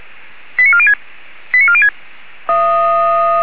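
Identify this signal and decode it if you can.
Five-tone selective call (ZVEI), heard as quick runs of five tones, tones of about 70 ms. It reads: 89406 89406
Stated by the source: ZVEI five-tone selective call received on an SDR.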